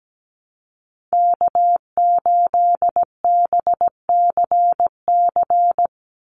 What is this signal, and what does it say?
Morse code, heard as a keyed tone, about 17 wpm, 700 Hz. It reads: K8BCC
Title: USA Call Signs 17wpm